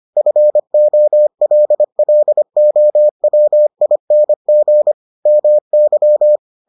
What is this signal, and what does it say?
Morse code, heard as a keyed tone, about 25 wpm, 600 Hz. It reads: FOLLOWING MY